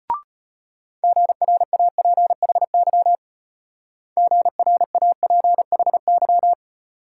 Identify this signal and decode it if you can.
Morse code, heard as a keyed tone, about 33 wpm, 700 Hz. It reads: GRAPHY GRAPHY